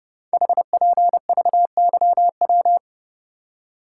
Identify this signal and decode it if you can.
Morse code, heard as a keyed tone, about 30 wpm, 700 Hz. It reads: HPVYW
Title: Code Groups - 5 Letters - 30wpm